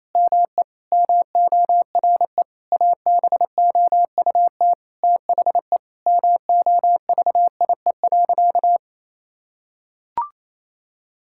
Morse code, heard as a keyed tone, about 28 wpm, 700 Hz. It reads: ME MORE ABOUT THE MOVIE.
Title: Sentences from Top 1000 Words - Review Part 5 - 28wpm